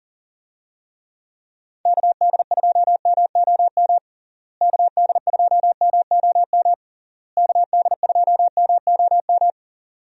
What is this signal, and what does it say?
Morse code, heard as a keyed tone, about 37 wpm, 700 Hz. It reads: KD2MOM KD2MOM KD2MOM
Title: USA Call Signs - 6 Characters - Repeated 3x - 40wpm